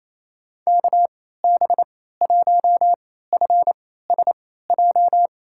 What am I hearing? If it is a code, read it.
Morse code, heard as a keyed tone, about 28 wpm, spaced as for 16 wpm, 700 Hz. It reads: KB1FSJ
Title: USA Call Signs - 3x Character Spacing - 28wpm